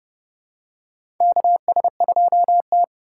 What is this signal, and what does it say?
Morse code, heard as a keyed tone, about 30 wpm, 700 Hz. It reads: KS2T
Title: USA Call Signs - 4 Characters - 2x Character Spacing - 30wpm